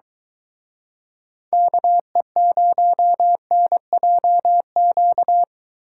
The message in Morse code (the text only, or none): KE0NJQ